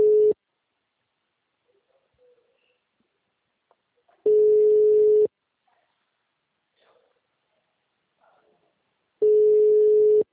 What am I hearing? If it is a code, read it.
Ringback tone (Europe, 425 Hz).
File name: out- -2023- - - .32